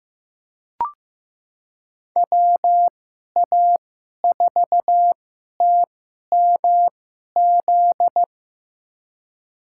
Morse code, heard as a keyed tone, about 15 wpm, spaced as for 11 wpm, 700 Hz. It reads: WA4TMZ
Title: USA Call Signs - 6 Characters - 2x Character Spacing - 15wpm